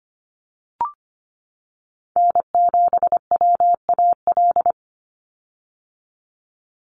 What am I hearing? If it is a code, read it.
Morse code, heard as a keyed tone, about 25 wpm, 700 Hz. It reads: N7WAL
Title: USA Call Signs 25wpm